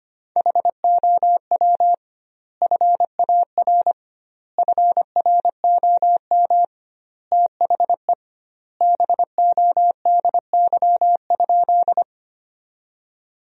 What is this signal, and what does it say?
Morse code, heard as a keyed tone, about 25 wpm, 700 Hz. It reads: HOW FAR FROM THE BODY?
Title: 2x Word-Spacing - Sentences from Top 400 Words - 25wpm